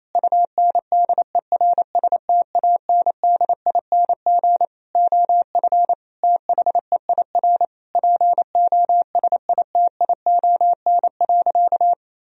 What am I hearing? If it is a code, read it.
Morse code, heard as a keyed tone, about 28 wpm, 700 Hz. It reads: UNDERSTANDING OF THEIR POSITION.
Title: Sentences from Top 1000 Words - Review Part 3 - 28wpm